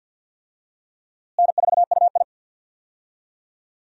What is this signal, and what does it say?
Morse code, heard as a keyed tone, about 46 wpm, 700 Hz. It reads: N4UI
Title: USA Call Signs - 4 Characters - Repeated 2x - 50wpm